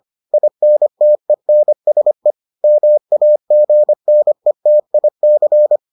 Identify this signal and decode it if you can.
Morse code, heard as a keyed tone, about 25 wpm, 600 Hz. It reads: INTENSE MAGNETIC